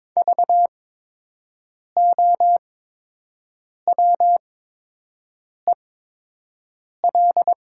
Morse code, heard as a keyed tone, about 22 wpm, spaced as for 6 wpm, 700 Hz. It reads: VOWEL
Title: 8x Character Spacing - Top 500 Words - 22wpm